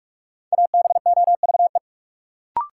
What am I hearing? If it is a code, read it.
Morse code, heard as a keyed tone, about 44 wpm, 700 Hz. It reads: ABOVE
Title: Sentences from Top 1000 Words - Review Part 2 - New Word Drill - 45wpm